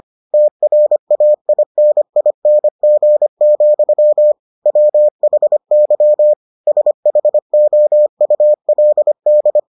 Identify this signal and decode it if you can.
Morse code, heard as a keyed tone, about 25 wpm, 600 Hz. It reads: TRAINING, WHY SHOULD